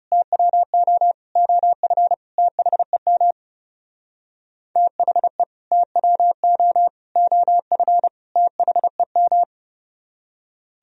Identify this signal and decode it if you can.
Morse code, heard as a keyed tone, about 36 wpm, 700 Hz. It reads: TWO OF THEM THE TWO OF THEM
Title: Speed-Racing - Sets of 4 Words 30wpm